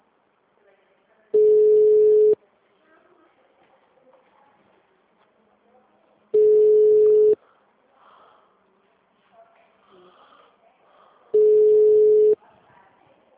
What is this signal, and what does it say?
Ringback tone (Europe, 425 Hz), with speech.
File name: out- -2016- - - .6848